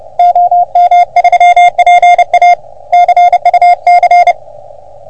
Morse code, heard as a keyed tone, about 30 wpm, 720 Hz, with radio noise. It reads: OM3PA CUC
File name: OM3PA PILUP CUC 388